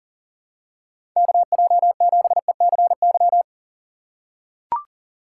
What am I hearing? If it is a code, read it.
Morse code, heard as a keyed tone, about 40 wpm, 700 Hz. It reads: KJ7ECY
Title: USA Call Signs - 6 Characters - 4x Character Spacing - 40wpm